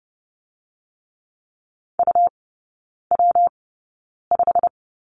Morse code, heard as a keyed tone, about 30 wpm, 700 Hz. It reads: U W 5
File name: Lesson 19e - Letter w - Three Characters 30wpm